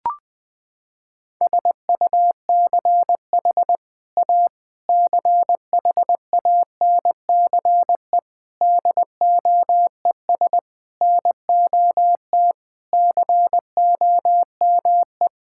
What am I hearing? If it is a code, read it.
Morse code, heard as a keyed tone, about 21 wpm, 700 Hz. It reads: SUCH A CHANCE DOES NOT COME
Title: Sentences from Top 1600 Words 20wpm